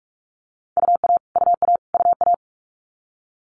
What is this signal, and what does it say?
Morse code, heard as a keyed tone, about 46 wpm, 700 Hz. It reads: UA UA UA
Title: Multi-letter Phonemes Repeated 3x 45wpm